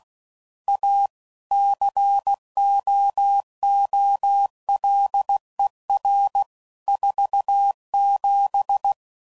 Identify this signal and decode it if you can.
Morse code, heard as a keyed tone, about 16 wpm, 800 Hz. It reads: A COOLER 47
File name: CW Code Groups @15WPM / Weather Sets / Weather 5 5